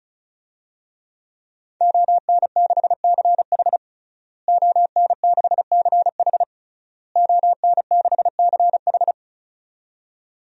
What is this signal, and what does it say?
Morse code, heard as a keyed tone, about 36 wpm, 700 Hz. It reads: ON6CH ON6CH ON6CH
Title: DXCC-100 Call Signs - 5 Characters - Repeated 3x - 35wpm